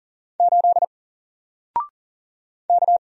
Morse code, heard as a keyed tone, about 39 wpm, 700 Hz. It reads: 8 K
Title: Single Character (Letter:Number:Punctuation:Prosign) 40wpm